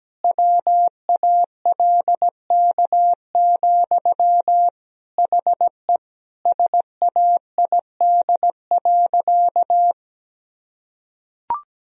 Morse code, heard as a keyed tone, about 17 wpm, 700 Hz. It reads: WALK, HE SAID.
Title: Sentences from Top 1000 Words - Review Part 1 - 17wpm